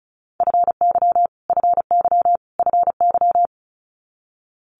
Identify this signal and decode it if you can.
Morse code, heard as a keyed tone, about 36 wpm, 700 Hz. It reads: FY FY FY
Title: Multi-letter Phonemes Repeated 3x 35wpm